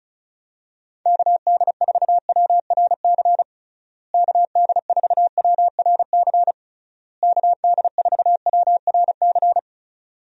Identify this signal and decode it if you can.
Morse code, heard as a keyed tone, about 36 wpm, 700 Hz. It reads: KD4WRC KD4WRC KD4WRC
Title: USA Call Signs - 6 Characters - Repeated 3x - 35wpm